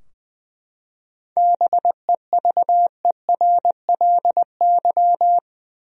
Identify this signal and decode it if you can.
Morse code, heard as a keyed tone, about 20 wpm, 700 Hz. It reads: BEVERLY
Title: USA Common Names - 20wpm